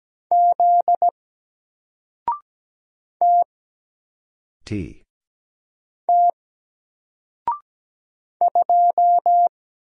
Morse code, heard as a keyed tone, about 17 wpm, 700 Hz. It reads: Z T T 2
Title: Single Letter-Number Alt 17wpm